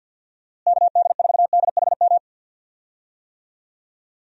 Morse code, heard as a keyed tone, about 51 wpm, 700 Hz. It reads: KD4DHM
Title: USA Call Signs - 6 Characters - Repeated 2x - 50wpm